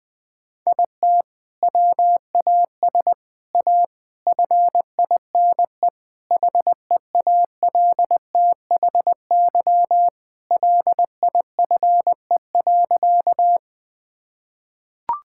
Morse code, heard as a keyed tone, about 21 wpm, 700 Hz. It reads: IT WAS A FINE HEALTHY LIFE.
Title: Sentences from Top 1000 Words - Review Part 5 - 20wpm